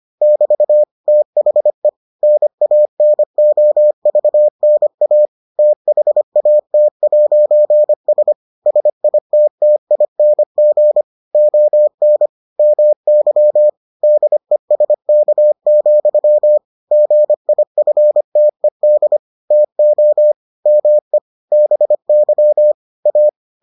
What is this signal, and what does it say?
Morse code, heard as a keyed tone, about 25 wpm, 600 Hz. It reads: <BT> THE NANOVNA THAT'S SITTING ON MY DESK, GIFTED TO ME BY A